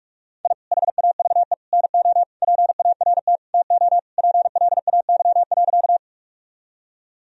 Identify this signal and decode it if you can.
Morse code, heard as a keyed tone, about 44 wpm, 700 Hz. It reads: I HAVE NO PART TO PLAY.
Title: Sentences from Top 1000 Words - Review Part 1 - 45wpm